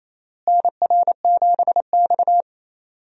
Morse code, heard as a keyed tone, about 28 wpm, 700 Hz. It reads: NR7X
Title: USA Call Signs - 4 Characters - Repeated 3x - 28wpm